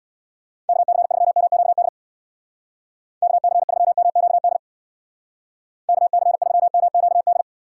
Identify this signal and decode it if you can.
Morse code, heard as a keyed tone, about 59 wpm, 700 Hz. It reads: KC3MQD KC3MQD KC3MQD
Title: Speed Racing - USA Call Signs 50wpm